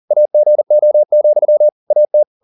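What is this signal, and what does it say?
Morse code, heard as a keyed tone, about 40 wpm, 600 Hz. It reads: AGO, AT
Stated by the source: Morse code at 40 wpm, practice text, CW med Gnister.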